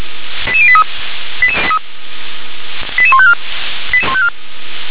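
Five-tone selective call (ZVEI), heard as quick runs of five tones, tones of about 70 ms. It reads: 88083 88154
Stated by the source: SDR-received ZVEI five-tone selective call, code 88148.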